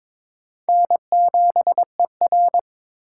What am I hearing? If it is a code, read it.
Morse code, heard as a keyed tone, about 21 wpm, 700 Hz. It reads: N7ER
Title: USA Call Signs - 4 Characters - 4x Character Spacing - 22wpm